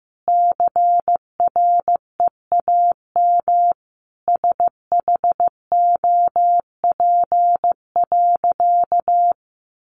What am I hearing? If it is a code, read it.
Morse code, heard as a keyed tone, about 14 wpm, 700 Hz. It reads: CREAM SHOP.
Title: Additional Easy Phrases 15wpm